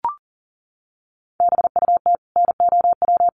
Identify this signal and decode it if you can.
Morse code, heard as a keyed tone, about 40 wpm, 700 Hz. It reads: BUT NOW
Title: Sentences from Top 1500 Words 40wpm